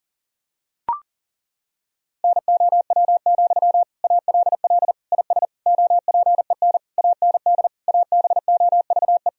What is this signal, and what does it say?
Morse code, heard as a keyed tone, about 40 wpm, 700 Hz. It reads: NOW, ALL IS OPEN AND ABOVE
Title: Sentences from Top 1000 Words - Review Part 4 - 40wpm